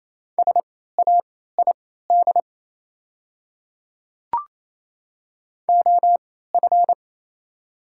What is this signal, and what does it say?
Morse code, heard as a keyed tone, about 28 wpm, spaced as for 16 wpm, 700 Hz. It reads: SAID OF
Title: 3x Character Spacing - Top 100 Words - 28wpm